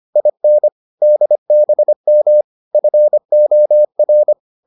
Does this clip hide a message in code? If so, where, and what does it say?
Yes: Morse code from 0.2 to 4.3 s, heard as a keyed tone, about 25 wpm, 600 Hz. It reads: IN DBM FOR